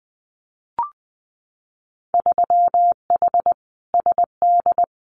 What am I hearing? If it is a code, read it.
Morse code, heard as a keyed tone, about 20 wpm, 700 Hz. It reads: 3H SD
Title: Winter Field Day Exchanges Encore 20wpm